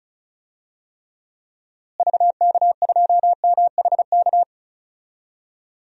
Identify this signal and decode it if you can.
Morse code, heard as a keyed tone, about 36 wpm, 700 Hz. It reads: VK2MHK
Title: DXCC-100 Call Signs - 6 Characters - Repeated 2x - 35wpm